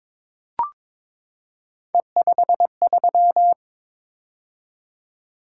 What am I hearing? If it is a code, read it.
Morse code, heard as a keyed tone, about 23 wpm, 700 Hz. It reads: E53
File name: practice / 22wpm / Three Letter-Number Combinations - 22wpm